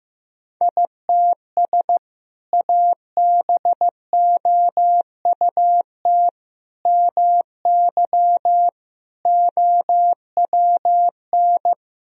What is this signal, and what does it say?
Morse code, heard as a keyed tone, about 16 wpm, 700 Hz. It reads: ITS ABOUT MY OWN
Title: Sentences from Top 1000 Words - Review Part 1 - 15wpm